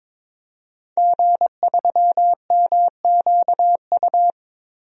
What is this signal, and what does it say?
Morse code, heard as a keyed tone, about 22 wpm, 700 Hz. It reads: G3MQU
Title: DXCC-100 Call Signs - 5 Characters - 22wpm